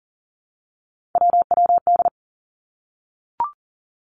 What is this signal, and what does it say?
Morse code, heard as a keyed tone, about 40 wpm, 700 Hz. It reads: WWD
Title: Three Challenging Letter-Number Combinations 40wpm